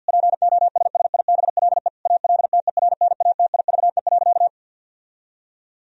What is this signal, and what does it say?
Morse code, heard as a keyed tone, about 51 wpm, 700 Hz. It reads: POSSIBLE ALTERNATIVE.
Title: Sentences from Top 1000 Words - Review Part 4 - 50wpm